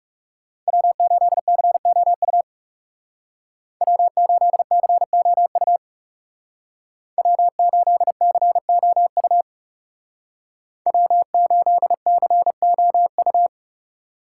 Morse code, heard as a keyed tone, about 34 wpm, 700 Hz. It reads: W8COU W8COU W8COU W8COU
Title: Speed Racing - USA 1x3 Call Signs 30wpm